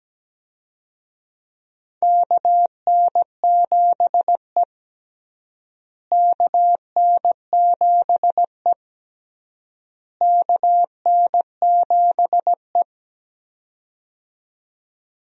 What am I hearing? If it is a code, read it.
Morse code, heard as a keyed tone, about 17 wpm, 700 Hz. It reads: KN7E KN7E KN7E
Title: USA Call Signs - 4 Characters - Repeated 3x - 17wpm